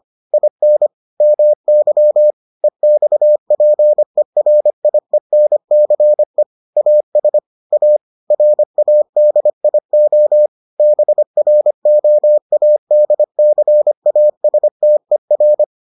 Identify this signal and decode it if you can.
Morse code, heard as a keyed tone, about 25 wpm, 600 Hz. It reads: IN MY EXPERIENCE AS A RADIO BROADCASTER